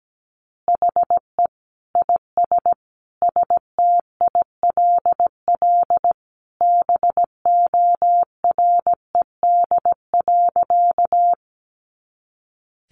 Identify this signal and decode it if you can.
Morse code, heard as a keyed tone, about 17 wpm, 700 Hz. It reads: HE IS STILL BORED.
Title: Easy Phrases 17wpm